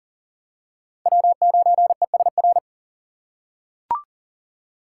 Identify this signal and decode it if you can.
Morse code, heard as a keyed tone, about 40 wpm, 700 Hz. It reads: W9ESR
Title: USA Call Signs - 5 Characters - 6x Character Spacing - 40wpm